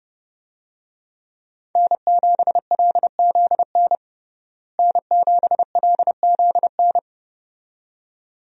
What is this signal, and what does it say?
Morse code, heard as a keyed tone, about 30 wpm, 700 Hz. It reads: N7LZN N7LZN
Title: USA Call Signs - 5 Characters - Repeated 2x - 30wpm